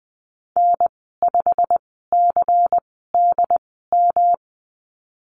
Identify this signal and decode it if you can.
Morse code, heard as a keyed tone, about 20 wpm, spaced as for 15 wpm, 700 Hz. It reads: N5CDM
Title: 167 Lesson 27g - Number 7 - Call Signs 20wpm